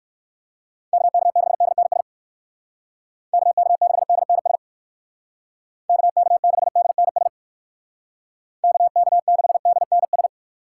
Morse code, heard as a keyed tone, about 59 wpm, 700 Hz. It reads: KK6DNS KK6DNS KK6DNS KK6DNS